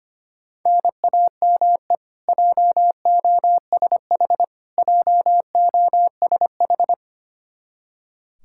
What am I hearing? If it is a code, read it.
Morse code, heard as a keyed tone, about 25 wpm, 700 Hz. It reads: NAME JOSH JOSH
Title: Practice QSO - Exchange 1 - Beginning - 25wpm